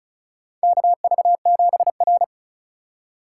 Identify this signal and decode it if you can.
Morse code, heard as a keyed tone, about 36 wpm, 700 Hz. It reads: KV7R
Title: USA Call Signs - 4x Character Spacing - 35wpm